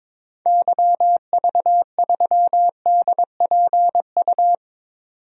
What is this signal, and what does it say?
Morse code, heard as a keyed tone, about 21 wpm, 700 Hz. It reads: YV3DPU
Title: DXCC-100 Call Signs - 6 Characters - 22wpm